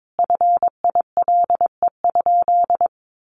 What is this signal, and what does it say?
Morse code, heard as a keyed tone, about 22 wpm, 700 Hz. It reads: FILE?
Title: Sentences from Top 1200 Words 22wpm